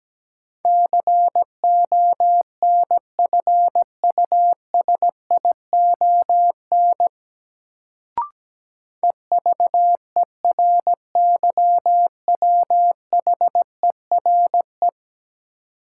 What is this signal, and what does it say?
Morse code, heard as a keyed tone, about 17 wpm, 700 Hz. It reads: CONFUSION EVERYWHERE
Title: Sentences from Top 1600 Words - New Word Drill - 17wpm